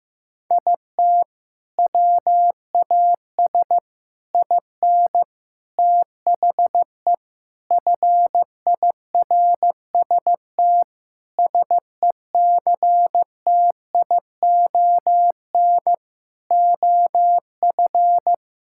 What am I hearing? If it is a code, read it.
Morse code, heard as a keyed tone, about 16 wpm, 700 Hz. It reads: IT WAS IN THE FIRST SECTION OF
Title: Sentences from Top 1000 Words - Review Part 5 - 15wpm